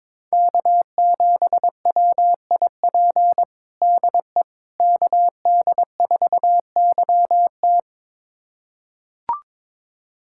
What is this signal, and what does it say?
Morse code, heard as a keyed tone, about 23 wpm, 700 Hz. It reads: K7WIP DE KD4YT
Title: USA Callsign DE Callsign - 5 Characters - 22wpm